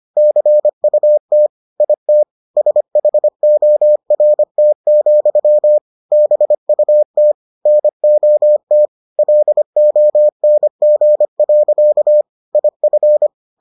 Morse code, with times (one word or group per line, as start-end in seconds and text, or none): 0.2-1.5 CUT
1.8-2.2 IT
2.6-5.8 SHORT,
6.1-7.3 BUT
7.7-8.9 NOT
9.2-12.2 LONG.
12.6-13.3 IF